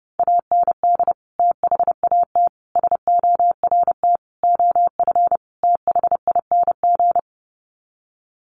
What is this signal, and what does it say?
Morse code, heard as a keyed tone, about 30 wpm, 700 Hz. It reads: AND THAT SORT OF THING
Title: Sets of 5 Words Encore2 30wpm